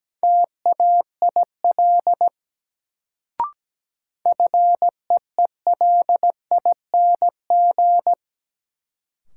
Morse code, heard as a keyed tone, about 17 wpm, 700 Hz. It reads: TAIL FEELING
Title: Sentences from Top 1000 Words - Review Part 3 - New Word Drill - 17wpm